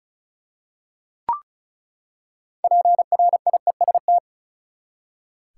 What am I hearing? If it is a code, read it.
Morse code, heard as a keyed tone, about 36 wpm, 700 Hz. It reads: PRIEST
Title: Six Letter Words - 35wpm